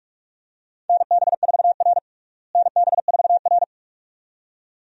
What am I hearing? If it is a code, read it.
Morse code, heard as a keyed tone, about 46 wpm, 700 Hz. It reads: NB4R NB4R
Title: USA Call Signs - 4 Characters - Repeated 2x - 45wpm